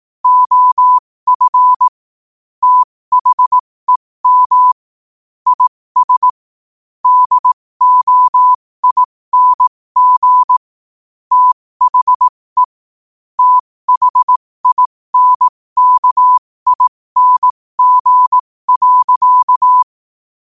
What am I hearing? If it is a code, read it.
Morse code, heard as a keyed tone, about 18 wpm, spaced as for 15 wpm, 1000 Hz. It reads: OF THEM IS DOING THE THINKING.
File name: qotdfiles / 15wpm